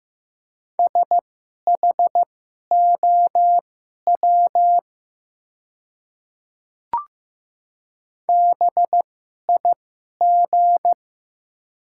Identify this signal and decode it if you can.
Morse code, heard as a keyed tone, about 15 wpm, spaced as for 11 wpm, 700 Hz. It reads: SHOW BIG
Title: 2x Character Spacing - Top 300 Words - 15wpm